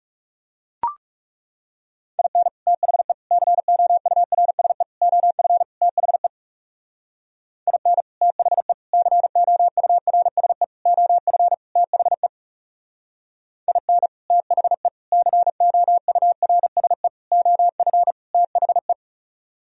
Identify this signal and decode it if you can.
Morse code, heard as a keyed tone, about 38 wpm, 700 Hz. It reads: IN THE COURSE OF THE IN THE COURSE OF THE IN THE COURSE OF THE
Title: Speed-Racing - Sets of 5 Words 30wpm